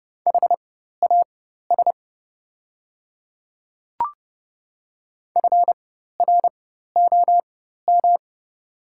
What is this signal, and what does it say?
Morse code, heard as a keyed tone, about 30 wpm, spaced as for 14 wpm, 700 Hz. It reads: HAS FROM
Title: 4x Character Spacing - Top 100 Words - 30wpm